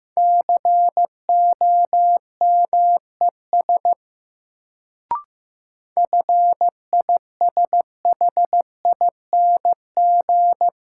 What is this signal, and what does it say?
Morse code, heard as a keyed tone, about 16 wpm, 700 Hz. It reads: COMES FISHING
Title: Sentences from Top 1000 Words - Review Part 5 - New Word Drill - 15wpm